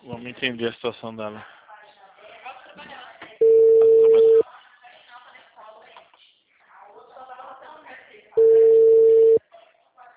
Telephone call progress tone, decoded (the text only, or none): ringback tone